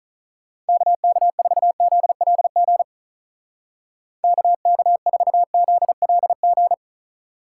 Morse code, heard as a keyed tone, about 46 wpm, 700 Hz. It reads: KK4ZLG KK4ZLG